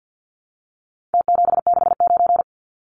Morse code, heard as a keyed tone, about 52 wpm, 700 Hz. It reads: T769
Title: Numbers - Contest Style - 4-Digit 50wpm